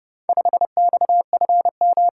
Morse code, heard as a keyed tone, about 30 wpm, 700 Hz. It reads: 5XFM